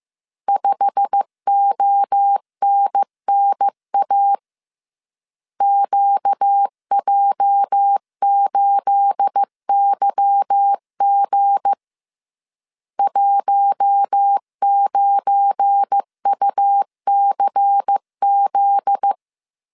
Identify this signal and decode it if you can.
Morse code, heard as a keyed tone, about 15 wpm, 800 Hz. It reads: 5ONNA QJ8YG 19UCZ
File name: CW / DET / 1 9 1500C S A 14wpm 3222